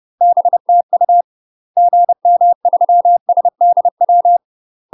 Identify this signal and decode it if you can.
Morse code, heard as a keyed tone, about 30 wpm, 700 Hz. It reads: BTU GM3SDW
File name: UK MASTER QSOs TRAINING / 30WPM / UK MASTER QSOs 0018